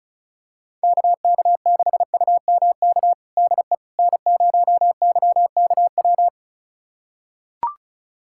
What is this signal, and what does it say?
Morse code, heard as a keyed tone, about 36 wpm, 700 Hz. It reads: KK6UMK DE N0YKW
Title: USA Callsign DE Callsign 35wpm